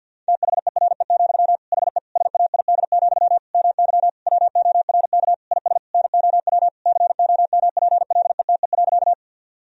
Morse code, heard as a keyed tone, about 49 wpm, 700 Hz. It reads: THERE, HE SAID, MY WORK IS NOW COMPLETE.
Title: Sentences from Top 1000 Words - Review Part 2 - 50wpm